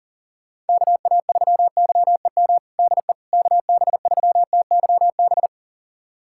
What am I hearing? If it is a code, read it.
Morse code, heard as a keyed tone, about 40 wpm, 700 Hz. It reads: KA3YEM DE KB3TYB
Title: USA Callsign DE Callsign 40wpm